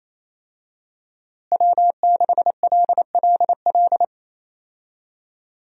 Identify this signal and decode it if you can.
Morse code, heard as a keyed tone, about 28 wpm, 700 Hz. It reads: W6LLL
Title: USA Call Signs - 5 Characters - Repeated 3x - 28wpm